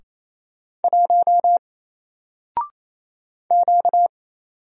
Morse code, heard as a keyed tone, about 27 wpm, 700 Hz. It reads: 1 Q
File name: Single-Letter-Number-28wpm